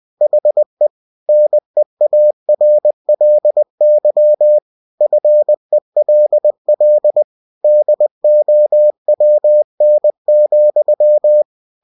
Morse code, heard as a keyed tone, about 20 wpm, 600 Hz. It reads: HE NEARLY FELL DOWN,
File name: cw-books / winnie-the-pooh / winnie-20wpm-page-0109